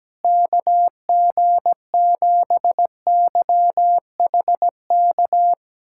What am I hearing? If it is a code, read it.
Morse code, heard as a keyed tone, about 17 wpm, 700 Hz. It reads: KG7YHK
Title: USA Callsign DE Callsign 17wpm